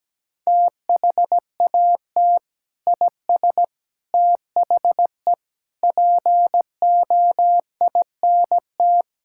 Morse code, heard as a keyed tone, about 17 wpm, 700 Hz. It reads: THAT IS THE POINT
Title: Sentences from Top 1000 Words - Review Part 5 - 17wpm